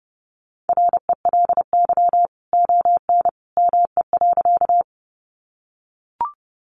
Morse code, heard as a keyed tone, about 30 wpm, 700 Hz. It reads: RELY ON ME.
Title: Sentences from Top 1200 Words 30wpm